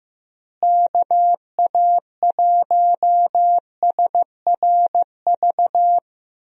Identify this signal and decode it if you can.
Morse code, heard as a keyed tone, about 15 wpm, 700 Hz. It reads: KA1SRV
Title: USA Call Signs - 6 Characters - Repeated 3x - 15wpm